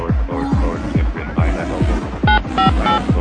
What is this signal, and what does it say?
DTMF, heard as telephone touch tones, tones of about 130 ms. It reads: 958